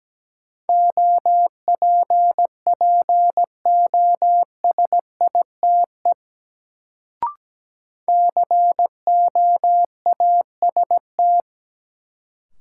Morse code, heard as a keyed tone, about 17 wpm, 700 Hz. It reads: OPPOSITE COAST
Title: Sentences from Top 1300 Words - New Word Drill Alt - 17wpm